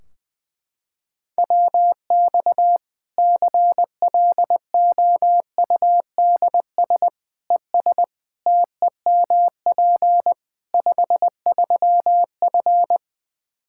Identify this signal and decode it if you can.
Morse code, heard as a keyed tone, about 20 wpm, 700 Hz. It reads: WX CLOUDS ES TEMP 53F
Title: Practice QSO - Exchange 2 - Info - 20wpm